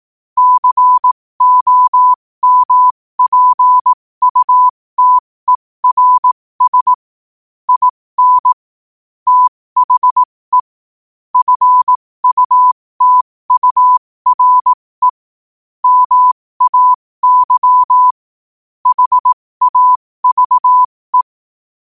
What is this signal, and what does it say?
Morse code, heard as a keyed tone, about 18 wpm, spaced as for 15 wpm, 1000 Hz. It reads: COMPUTERS IN THE FUTURE MAY HAVE
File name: qotdfiles / 15wpm